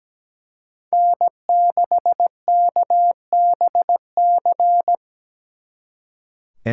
Morse code, heard as a keyed tone, about 20 wpm, 700 Hz. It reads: N6KBC
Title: Speed Racing - USA 1x3 Call Signs 17wpm